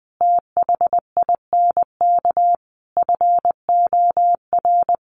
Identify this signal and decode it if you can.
Morse code, heard as a keyed tone, about 20 wpm, 700 Hz. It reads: THINK FOR
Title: Sentences from Top 1000 Words 20wpm